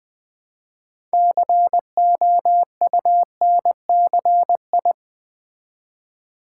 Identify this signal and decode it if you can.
Morse code, heard as a keyed tone, about 20 wpm, 700 Hz. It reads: COUNCI
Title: Build A Word - 3 Letters At A Time - 9-Letter Words - 20wpm